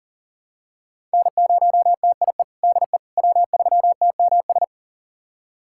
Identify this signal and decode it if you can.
Morse code, heard as a keyed tone, about 40 wpm, 700 Hz. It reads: N0TIE DE W3TMS
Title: USA Callsign DE Callsign - 5 Characters - 40wpm